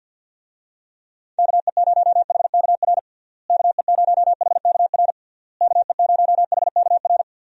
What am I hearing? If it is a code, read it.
Morse code, heard as a keyed tone, about 48 wpm, 700 Hz. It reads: KE0HKR KE0HKR KE0HKR
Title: USA Call Signs - 6 Characters - Repeated 3x - 50wpm